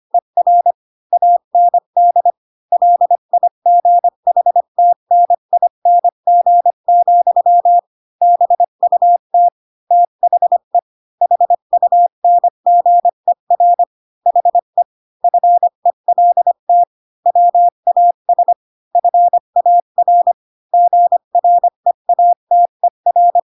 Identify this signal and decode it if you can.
Morse code, heard as a keyed tone, about 25 wpm, 700 Hz. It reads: ER AND LIGHTNING, BUT THE HUNGER HE FELT WAS FAR GREATER